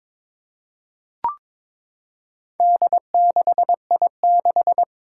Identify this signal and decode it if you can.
Morse code, heard as a keyed tone, about 21 wpm, 700 Hz. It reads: D6I6
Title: Four Challenging Letter-Number Combinations 22wpm